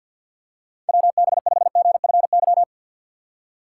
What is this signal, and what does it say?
Morse code, heard as a keyed tone, about 53 wpm, 700 Hz. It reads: WB5GFY
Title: USA Call Signs - 6 Characters - 3x Character Spacing - 50wpm